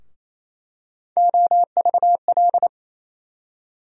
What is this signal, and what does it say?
Morse code, heard as a keyed tone, about 28 wpm, 700 Hz. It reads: OVL
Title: USA Call Sign Postfixes - 3 Letters 28wpm